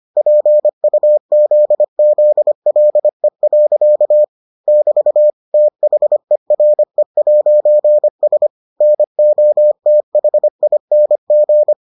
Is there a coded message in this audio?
Yes: Morse code from 0.2 to 11.7 s, heard as a keyed tone, about 25 wpm, 600 Hz. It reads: PUZZLE. <BT> THERE'S NOTHING